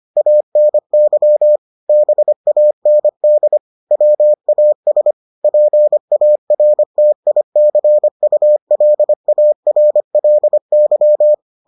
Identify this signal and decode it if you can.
Morse code, heard as a keyed tone, about 25 wpm, 600 Hz. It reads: ANY BAND WAS PARTICULARLY